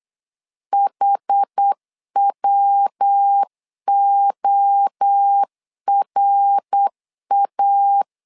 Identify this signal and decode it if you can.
Morse code, heard as a keyed tone, about 8 wpm, 800 Hz. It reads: HWORA
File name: LEZIONI / 4 Q1S 8wpm 2710